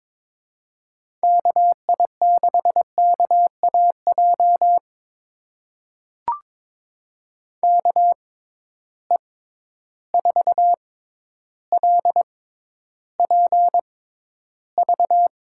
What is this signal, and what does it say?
Morse code, heard as a keyed tone, about 22 wpm, spaced as for 11 wpm, 700 Hz. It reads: KI6KAJ KE4LPV
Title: USA Call Signs - 6 Characters - 6x Character Spacing - 22wpm